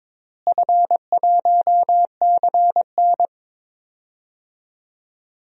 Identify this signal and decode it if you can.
Morse code, heard as a keyed tone, about 22 wpm, 700 Hz. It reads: F1CN
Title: DXCC-100 Call Signs - 4 Characters - Repeated 3x - 22wpm